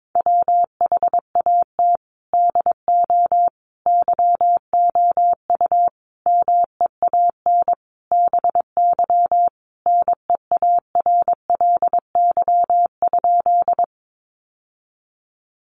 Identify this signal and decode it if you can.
Morse code, heard as a keyed tone, about 22 wpm, 700 Hz. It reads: WHAT DO YOU MEAN BY NEARLY?
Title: Sentences from Top 1300 Words 22wpm